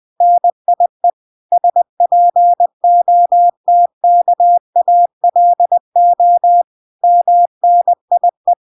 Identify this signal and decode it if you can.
Morse code, heard as a keyed tone, about 20 wpm, 700 Hz. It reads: NIE SPOTKALO MNIE